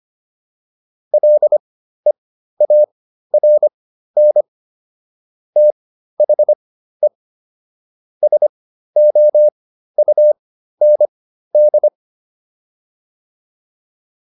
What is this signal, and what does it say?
Morse code, heard as a keyed tone, about 25 wpm, spaced as for 13 wpm, 600 Hz. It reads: LEARN THE SOUND